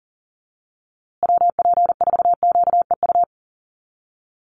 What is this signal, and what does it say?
Morse code, heard as a keyed tone, about 40 wpm, 700 Hz. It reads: WP4QEU
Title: USA Call Signs Repeated 3x 40wpm